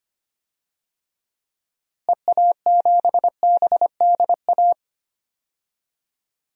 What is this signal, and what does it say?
Morse code, heard as a keyed tone, about 25 wpm, 700 Hz. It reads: EA7BDA
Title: DXCC-100 Call Signs - 6 Characters - Repeated 2x - 25wpm